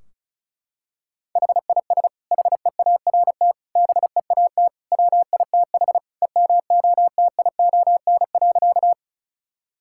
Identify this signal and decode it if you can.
Morse code, heard as a keyed tone, about 36 wpm, 700 Hz. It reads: HIS HEART BEAT WITH EMOTION.
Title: Sentences from Top 1600 Words 35wpm